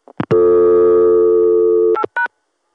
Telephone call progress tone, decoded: dial tone